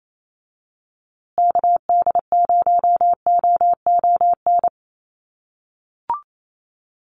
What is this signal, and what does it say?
Morse code, heard as a keyed tone, about 28 wpm, 700 Hz. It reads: KD0OON
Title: USA Call Signs Repeated 2x 28wpm